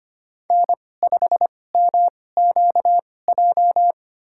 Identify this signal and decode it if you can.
Morse code, heard as a keyed tone, about 25 wpm, spaced as for 18 wpm, 700 Hz. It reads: N5MQJ
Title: USA Call Signs - 5 Characters - 2x Character Spacing - 25wpm